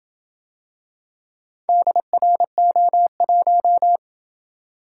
Morse code, heard as a keyed tone, about 26 wpm, 700 Hz. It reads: DRO1